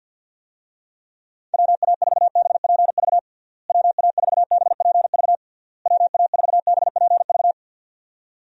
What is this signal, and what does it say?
Morse code, heard as a keyed tone, about 51 wpm, 700 Hz. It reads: WA4BPV WA4BPV WA4BPV
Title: USA Call Signs - 6 Characters - Repeated 3x - 50wpm